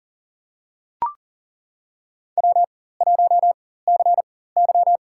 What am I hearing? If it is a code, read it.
Morse code, heard as a keyed tone, about 40 wpm, spaced as for 19 wpm, 700 Hz. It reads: W1CY
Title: USA Call Signs - 4 Characters - 4x Character Spacing - 40wpm